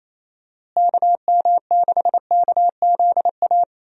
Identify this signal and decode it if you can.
Morse code, heard as a keyed tone, about 28 wpm, 700 Hz. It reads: KM6KZA
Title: USA Call Signs - 6 Characters - Repeated 2x - 28wpm